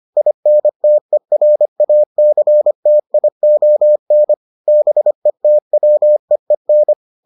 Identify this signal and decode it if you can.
Morse code, heard as a keyed tone, about 25 wpm, 600 Hz. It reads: INTERACTION BETWEEN